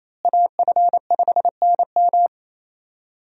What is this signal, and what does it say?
Morse code, heard as a keyed tone, about 28 wpm, 700 Hz. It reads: AF5NM
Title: USA Call Signs - 5 Characters - Repeated 2x - 28wpm